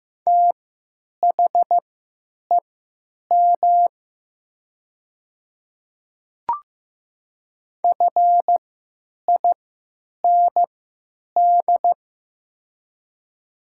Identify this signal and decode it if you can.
Morse code, heard as a keyed tone, about 15 wpm, spaced as for 9 wpm, 700 Hz. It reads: THEM FIND
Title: 3x Character Spacing - Top 200 Words - 15wpm